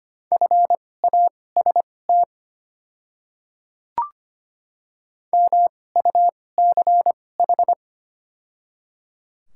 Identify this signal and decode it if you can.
Morse code, heard as a keyed tone, about 25 wpm, spaced as for 18 wpm, 700 Hz. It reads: FAST MUCH
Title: 2x Character Spacing - Top 500 Words - 25wpm